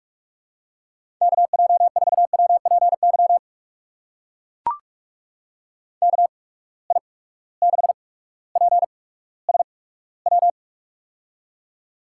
Morse code, heard as a keyed tone, about 44 wpm, spaced as for 19 wpm, 700 Hz. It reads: KJ4WPY KI6PSW
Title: USA Call Signs - 6 Characters - 8x Character Spacing - 45wpm